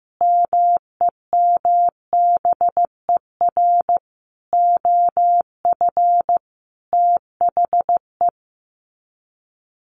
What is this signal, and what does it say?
Morse code, heard as a keyed tone, about 14 wpm, 700 Hz. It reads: MEMBER OF THE
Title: Sets of 4 Words 15wpm